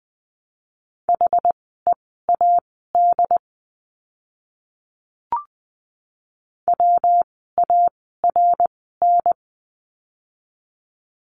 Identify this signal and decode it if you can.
Morse code, heard as a keyed tone, about 20 wpm, spaced as for 14 wpm, 700 Hz. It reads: HEAD WARN
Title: 115 Lesson 19f - Letter w - Words 20wpm